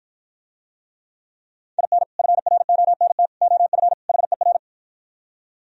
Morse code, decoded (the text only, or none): IN FRONT OF HER